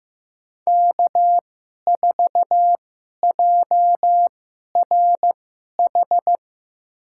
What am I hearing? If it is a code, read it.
Morse code, heard as a keyed tone, about 15 wpm, spaced as for 11 wpm, 700 Hz. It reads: K4JRH
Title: USA Call Signs - 5 Characters - 2x Character Spacing - 15wpm